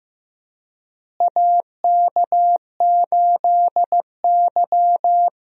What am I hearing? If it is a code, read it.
Morse code, heard as a keyed tone, about 15 wpm, 700 Hz. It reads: AK8Y
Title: USA Call Signs - 4 Characters - Repeated 2x - 15wpm